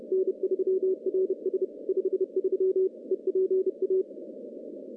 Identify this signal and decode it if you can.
Morse code, heard as a keyed tone, about 30 wpm, 380 Hz. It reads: N3RS 53 EPA
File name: CW-SS